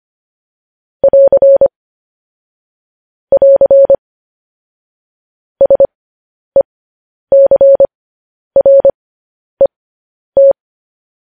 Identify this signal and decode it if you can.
Morse code, heard as a keyed tone, about 25 wpm, spaced as for 10 wpm, 550 Hz. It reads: <AR> <AR> SECRET